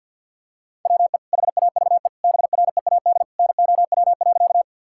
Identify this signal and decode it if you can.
Morse code, heard as a keyed tone, about 50 wpm, 700 Hz. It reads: WE HAVE BREAD NOW.
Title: Sentences from Top 1600 Words 50wpm